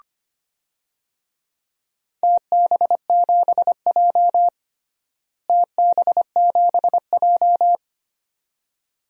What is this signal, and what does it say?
Morse code, heard as a keyed tone, about 25 wpm, 700 Hz. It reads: TB7J TB7J
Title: DXCC-100 Call Signs - 4 Characters - Repeated 2x - 25wpm